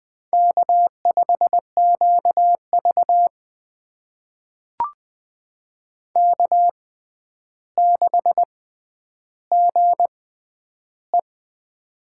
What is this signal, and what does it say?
Morse code, heard as a keyed tone, about 20 wpm, spaced as for 10 wpm, 700 Hz. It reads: K5QV K6GE
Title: USA Call Signs - 4 Characters - 6x Character Spacing - 20wpm